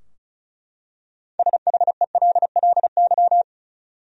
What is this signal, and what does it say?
Morse code, heard as a keyed tone, about 34 wpm, 700 Hz. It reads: SHELLY
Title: USA Common Names - 35wpm